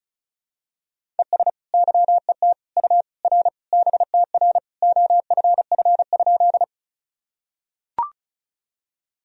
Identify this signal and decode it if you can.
Morse code, heard as a keyed tone, about 36 wpm, 700 Hz. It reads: ES YET U R BTR OFF?
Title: Sentences from Top 500 Words and Abbreviations 35wpm